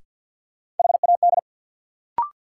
Morse code, heard as a keyed tone, about 53 wpm, 700 Hz. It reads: HAD
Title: Sets of 2 Words - New Word Drill - 50wpm